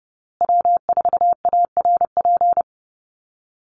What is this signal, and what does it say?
Morse code, heard as a keyed tone, about 30 wpm, 700 Hz. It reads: W4ARP
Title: USA Call Signs Repeated 3x 30wpm